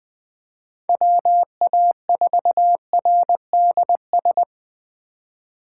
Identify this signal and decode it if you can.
Morse code, heard as a keyed tone, about 20 wpm, 700 Hz. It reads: WA4RDS
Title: USA Call Signs - 6 Characters - 2x Character Spacing - 20wpm